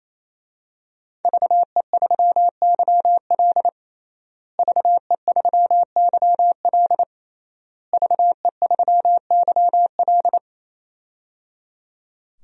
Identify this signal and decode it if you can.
Morse code, heard as a keyed tone, about 28 wpm, 700 Hz. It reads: VE3YL VE3YL VE3YL
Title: DXCC-100 Call Signs - 5 Characters - Repeated 3x - 28wpm